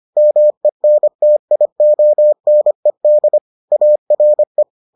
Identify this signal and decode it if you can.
Morse code, heard as a keyed tone, about 25 wpm, 600 Hz. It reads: MENTIONED ARE